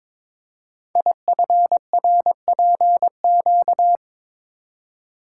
Morse code, heard as a keyed tone, about 22 wpm, 700 Hz. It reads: IFRPQ
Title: Code Groups - 5 Letter:Numbers - 22wpm